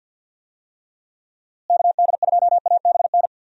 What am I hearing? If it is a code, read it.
Morse code, heard as a keyed tone, about 51 wpm, 700 Hz. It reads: KD2ABN
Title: USA Call Signs - 6 Characters - Repeated 2x - 50wpm